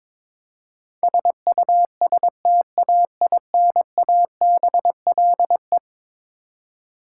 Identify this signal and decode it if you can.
Morse code, heard as a keyed tone, about 21 wpm, 700 Hz. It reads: SUSTAINABLE
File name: practice / 22wpm / Eleven Letter Words - 22wpm